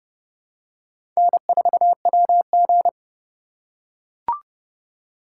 Morse code, heard as a keyed tone, about 31 wpm, 700 Hz. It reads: N4WG
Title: USA Call Signs - 4 Characters - 3x Character Spacing - 30wpm